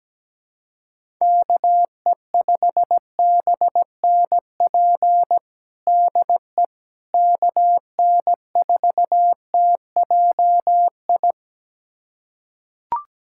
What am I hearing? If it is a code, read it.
Morse code, heard as a keyed tone, about 17 wpm, 700 Hz. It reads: KE5BNP DE KN4TJI
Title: USA Callsign DE Callsign 17wpm